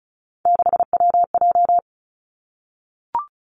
Morse code, heard as a keyed tone, about 36 wpm, 700 Hz. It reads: 6WJ
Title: Three Challenging Letter-Number Combinations 35wpm